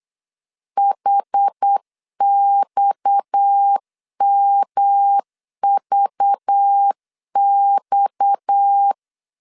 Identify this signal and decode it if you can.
Morse code, heard as a keyed tone, about 8 wpm, 800 Hz. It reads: HXMVX
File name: LEZIONI / 6 PKX0 / 6 PKX 8wpm 3041